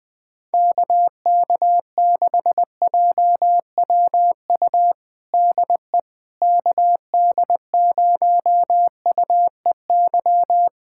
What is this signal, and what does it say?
Morse code, heard as a keyed tone, about 21 wpm, 700 Hz. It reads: KK6JWU DE KD0UEY
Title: USA Callsign DE Callsign 20wpm